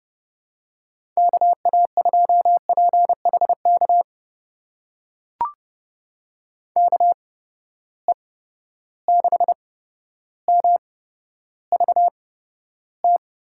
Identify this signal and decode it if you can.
Morse code, heard as a keyed tone, about 32 wpm, spaced as for 13 wpm, 700 Hz. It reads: KA2PHK KE6MVT
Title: USA Call Signs - 6 Characters - 8x Character Spacing - 30wpm